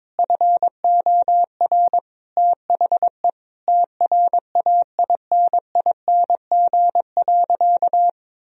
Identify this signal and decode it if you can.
Morse code, heard as a keyed tone, about 22 wpm, 700 Hz. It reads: FOR THE TRAINING.
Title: Sentences from Top 1000 Words - Review Part 4 - 22wpm